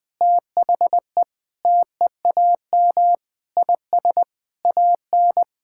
Morse code, heard as a keyed tone, about 20 wpm, 700 Hz. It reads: THE TEAM IS AN
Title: Sentences with 5-Word Phrases - 20wpm